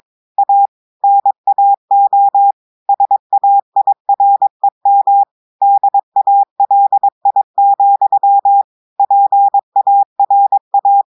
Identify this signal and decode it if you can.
Morse code, heard as a keyed tone, about 22 wpm, 800 Hz. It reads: A NAO SAIREM DALI, PARA
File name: audios / portugais / portugais texte en clair 2 vitesse 22 mots minutes